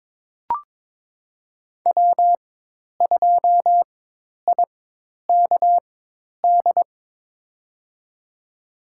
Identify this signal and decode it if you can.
Morse code, heard as a keyed tone, about 22 wpm, spaced as for 10 wpm, 700 Hz. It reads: W2IKD
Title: USA Call Signs - 5 Characters - 4x Character Spacing - 22wpm